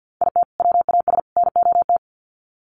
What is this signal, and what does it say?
Morse code, heard as a keyed tone, about 50 wpm, 700 Hz. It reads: IT WAS NOT
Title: Sets of 3 Words 50wpm